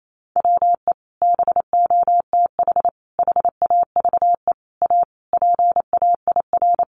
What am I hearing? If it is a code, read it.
Morse code, heard as a keyed tone, about 28 wpm, 700 Hz. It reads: WE BOTH HAVE A PAIR
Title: Sentences from Top 1200 Words 28wpm